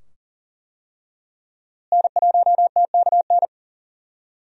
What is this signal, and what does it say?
Morse code, heard as a keyed tone, about 38 wpm, 700 Hz. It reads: N1TKN
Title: USA Call Signs - 5 Characters - Repeated 2x - 40wpm